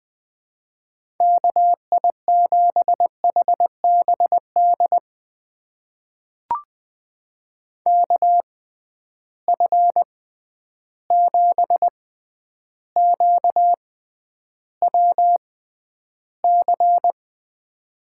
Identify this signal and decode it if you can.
Morse code, heard as a keyed tone, about 20 wpm, spaced as for 10 wpm, 700 Hz. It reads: KI7HBD KF7QWC
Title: USA Call Signs - 6 Characters - 6x Character Spacing - 20wpm